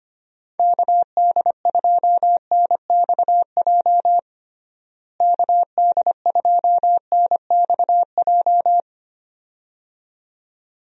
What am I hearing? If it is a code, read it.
Morse code, heard as a keyed tone, about 25 wpm, 700 Hz. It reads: KD2NXJ KD2NXJ
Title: USA Call Signs - 6 Characters - Repeated 2x - 25wpm